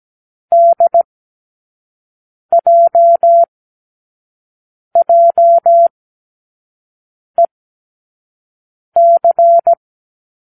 Morse code, heard as a keyed tone, about 17 wpm, spaced as for 5 wpm, 680 Hz. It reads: DJJEC
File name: k7qo / Lesson18(A-L)